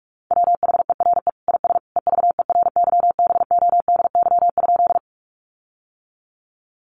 Morse code, heard as a keyed tone, about 44 wpm, 700 Hz. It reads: WHERE IS EVERYBODY?
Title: Sentences from Top 1400 Words 45wpm